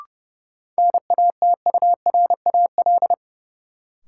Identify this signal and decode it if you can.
Morse code, heard as a keyed tone, about 29 wpm, 700 Hz. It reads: NATURAL
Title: Sentences from Top 700 Words - New Word Drill - 30wpm